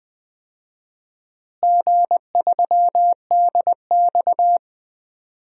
Morse code, heard as a keyed tone, about 20 wpm, 700 Hz. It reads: G3DX
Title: DXCC-100 Call Signs - 4 Characters - Repeated 2x - 20wpm